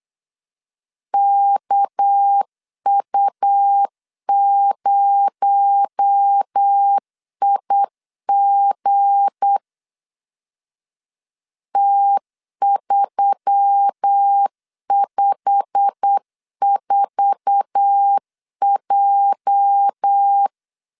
Morse code, heard as a keyed tone, about 8 wpm, 800 Hz. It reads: KU0IG T354J
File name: LEZIONI / 8 YJ45 / 8 YJ45 8wpm 3214